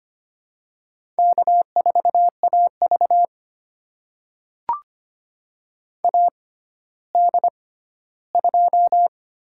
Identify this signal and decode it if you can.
Morse code, heard as a keyed tone, about 25 wpm, spaced as for 14 wpm, 700 Hz. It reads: K4AV AD2
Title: USA Call Signs - 4 Characters - 6x Character Spacing - 25wpm